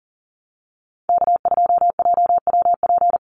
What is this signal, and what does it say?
Morse code, heard as a keyed tone, about 40 wpm, 700 Hz. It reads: K2JWP
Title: USA Call Signs Repeated 3x 40wpm